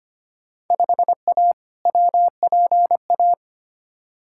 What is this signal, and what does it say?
Morse code, heard as a keyed tone, about 25 wpm, 700 Hz. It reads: 5A WPA
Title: Field Day Exchanges Encore - 25wpm